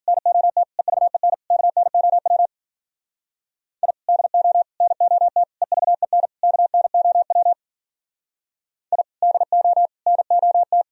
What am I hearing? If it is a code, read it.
Morse code, heard as a keyed tone, about 46 wpm, 700 Hz. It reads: NOT EVEN KNOW I DO NOT EVEN KNOW I DO NOT
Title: Speed-Racing - Sets of 5 Words 40wpm